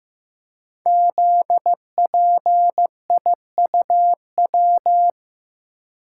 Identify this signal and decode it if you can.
Morse code, heard as a keyed tone, about 15 wpm, 700 Hz. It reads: ZPIUW
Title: Code Groups - 5 Letters - 15wpm